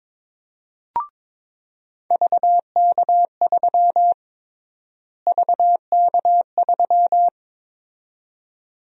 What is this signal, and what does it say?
Morse code, heard as a keyed tone, about 21 wpm, 700 Hz. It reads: VK3 VK3
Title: USA Call Sign Prefixes - Repeated 2x - 22wpm